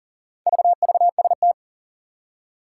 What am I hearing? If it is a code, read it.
Morse code, heard as a keyed tone, about 40 wpm, 700 Hz. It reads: VVST
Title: Four Challenging Letter-Number Combinations 40wpm